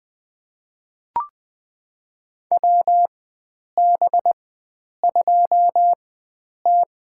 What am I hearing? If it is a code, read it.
Morse code, heard as a keyed tone, about 20 wpm, spaced as for 9 wpm, 700 Hz. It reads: WB2T
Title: USA Call Signs - 4x Character Spacing - 20wpm